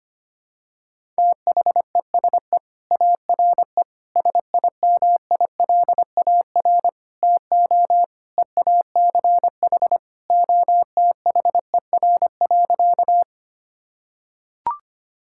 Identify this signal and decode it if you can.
Morse code, heard as a keyed tone, about 25 wpm, 700 Hz. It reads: THESE ARE SIMILAR TO EACH OTHER.
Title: Sentences from Top 1000 Words - Review Part 5 - 25wpm